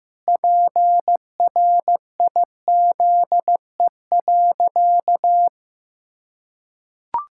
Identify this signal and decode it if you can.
Morse code, heard as a keyed tone, about 15 wpm, 700 Hz. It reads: PRIZE.
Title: Pangram Challenge v2 - 15wpm